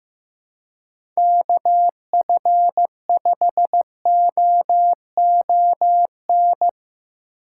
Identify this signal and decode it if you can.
Morse code, heard as a keyed tone, about 18 wpm, 700 Hz. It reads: KF5OON
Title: Speed Racing - USA 2x3 Call Signs 15wpm